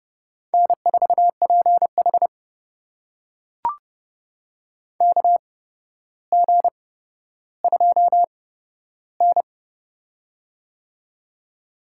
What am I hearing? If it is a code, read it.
Morse code, heard as a keyed tone, about 30 wpm, spaced as for 13 wpm, 700 Hz. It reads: N4PH KG2N
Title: USA Call Signs - 4 Characters - 8x Character Spacing - 30wpm